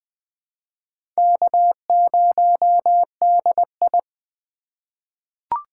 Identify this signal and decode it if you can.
Morse code, heard as a keyed tone, about 20 wpm, 700 Hz. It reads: K0DI
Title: USA Call Signs - 4x Character Spacing - 20wpm